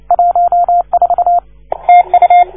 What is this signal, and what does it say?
Morse code, heard as a keyed tone, about 29 wpm, 690 Hz. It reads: 14 TU
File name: 1esempio-telegrafia